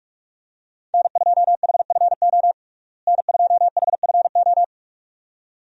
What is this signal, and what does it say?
Morse code, heard as a keyed tone, about 46 wpm, 700 Hz. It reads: N2HFO N2HFO
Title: USA Call Signs - 5 Characters - Repeated 2x - 45wpm